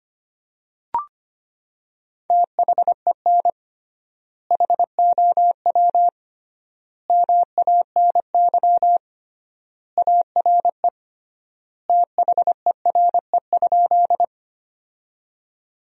Morse code, heard as a keyed tone, about 25 wpm, 700 Hz. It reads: THEN HOW MANY ARE THERE?
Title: 3x Word-Spacing - Sentences from Top 100 Words - 25wpm